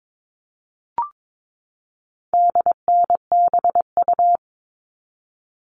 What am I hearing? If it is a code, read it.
Morse code, heard as a keyed tone, about 22 wpm, 700 Hz. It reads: DNBU
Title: All Cut Numbers - 4-Digit 22wpm